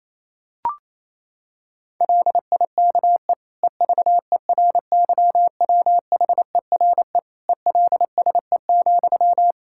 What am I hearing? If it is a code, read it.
Morse code, heard as a keyed tone, about 28 wpm, 700 Hz. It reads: LIKE EVERYWHERE ELSE,
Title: Sentences from Top 1600 Words 28wpm